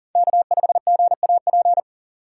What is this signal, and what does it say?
Morse code, heard as a keyed tone, about 40 wpm, 700 Hz. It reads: K5GAP
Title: USA Call Signs - 5 Characters - 3x Character Spacing - 40wpm